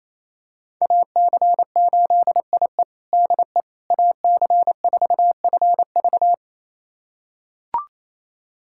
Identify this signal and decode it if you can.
Morse code, heard as a keyed tone, about 28 wpm, 700 Hz. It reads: AC8IE DE AC4FV
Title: USA Callsign DE Callsign - 5 Characters - 28wpm